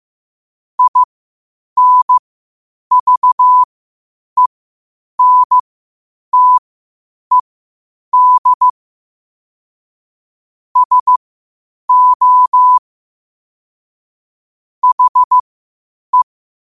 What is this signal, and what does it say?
Morse code, heard as a keyed tone, about 15 wpm, spaced as for 8 wpm, 1000 Hz. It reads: INVENTED SO HE
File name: qotdfiles / 8wpm